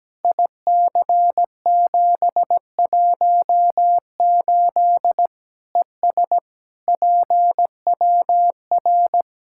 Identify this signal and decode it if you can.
Morse code, heard as a keyed tone, about 17 wpm, 700 Hz. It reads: IC718 ES PWR
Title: Practice QSO - Exchange 2 - Info - 17wpm